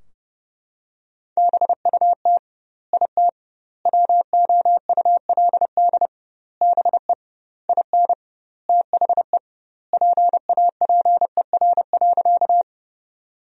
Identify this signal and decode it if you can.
Morse code, heard as a keyed tone, about 29 wpm, 700 Hz. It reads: BUT IT WOULD BE IN THE PAPER.
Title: 2x Word-Spacing - Sentences from Top 300 Words - 30wpm